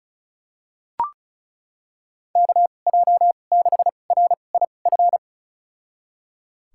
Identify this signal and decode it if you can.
Morse code, heard as a keyed tone, about 36 wpm, spaced as for 25 wpm, 700 Hz. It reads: KJ6RIF
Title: USA Call Signs - 6 Characters - 2x Character Spacing - 35wpm